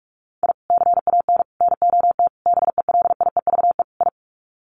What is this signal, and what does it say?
Morse code, heard as a keyed tone, about 44 wpm, 700 Hz. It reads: I CAN NOT BELIEVE I
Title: Sets of 5 Words Encore2 45wpm